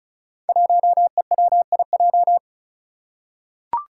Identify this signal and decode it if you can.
Morse code, heard as a keyed tone, about 36 wpm, 700 Hz. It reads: 1EWIJ
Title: Five Challenging Letter-Number Combinations 35wpm